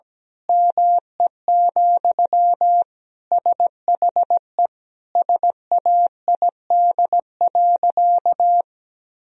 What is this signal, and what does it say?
Morse code, heard as a keyed tone, about 17 wpm, 700 Hz. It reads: ME, SHE SAID.
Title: Sentences from Top 1000 Words - Review Part 3 - 17wpm